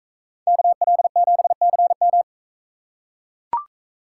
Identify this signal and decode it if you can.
Morse code, heard as a keyed tone, about 41 wpm, 700 Hz. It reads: KL7CM